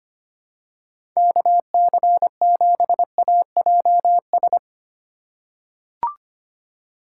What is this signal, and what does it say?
Morse code, heard as a keyed tone, about 25 wpm, 700 Hz. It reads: KC7AJS
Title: USA Call Signs - 6 Characters - 4x Character Spacing - 25wpm